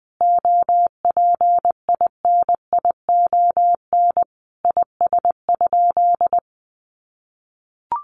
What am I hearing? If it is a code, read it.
Morse code, heard as a keyed tone, about 20 wpm, 700 Hz. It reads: OPINION IS?
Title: Sentences from Top 1200 Words 20wpm